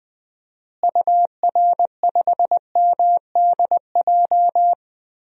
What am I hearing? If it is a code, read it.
Morse code, heard as a keyed tone, about 21 wpm, 700 Hz. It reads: UR5MDJ
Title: DXCC-100 Call Signs - 6 Characters - Repeated 3x - 20wpm